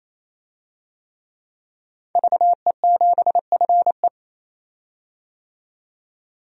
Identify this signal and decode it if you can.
Morse code, heard as a keyed tone, about 28 wpm, 700 Hz. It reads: VE7FE
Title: DXCC-100 Call Signs - 5 Characters - 28wpm